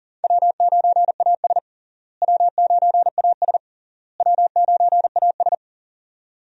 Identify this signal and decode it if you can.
Morse code, heard as a keyed tone, about 40 wpm, 700 Hz. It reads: W9AS W9AS W9AS
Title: USA Call Signs - 4 Characters - Repeated 3x - 40wpm